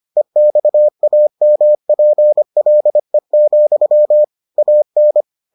Morse code, heard as a keyed tone, about 25 wpm, 600 Hz. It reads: EXAMPLE, AN